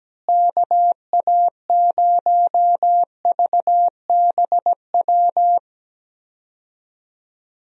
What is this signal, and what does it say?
Morse code, heard as a keyed tone, about 17 wpm, 700 Hz. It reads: KA0VBW
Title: USA Call Signs - 6 Characters - Repeated 2x - 17wpm